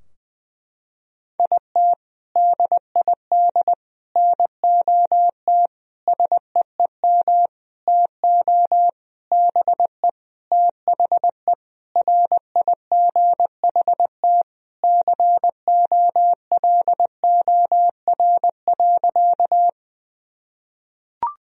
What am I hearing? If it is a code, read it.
Morse code, heard as a keyed tone, about 20 wpm, 700 Hz. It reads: IT DID NOT SEEM TO BE THE RIGHT COLOR.
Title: Sentences from Top 1000 Words - Review Part 2 - 20wpm